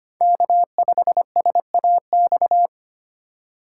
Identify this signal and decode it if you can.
Morse code, heard as a keyed tone, about 25 wpm, 700 Hz. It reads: K5SAX
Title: USA Call Signs - 5 Characters - 4x Character Spacing - 25wpm